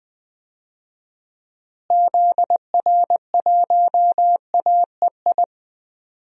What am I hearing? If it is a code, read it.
Morse code, heard as a keyed tone, about 20 wpm, 700 Hz. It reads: ZR1AEI
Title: DXCC-100 Call Signs - 6 Characters - Repeated 3x - 20wpm